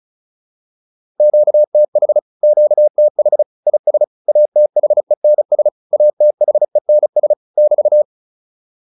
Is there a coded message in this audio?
Yes: Morse code from 1.2 to 8.0 s, heard as a keyed tone, about 34 wpm, 600 Hz. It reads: QTH QTH IS ATHENS ATHENS <BT>